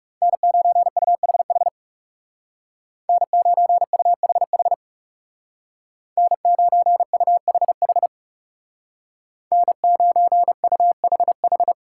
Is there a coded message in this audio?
Yes: Morse code from 0.2 to 11.7 s, heard as a keyed tone, about 39 wpm, 700 Hz. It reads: N9UHH N9UHH N9UHH N9UHH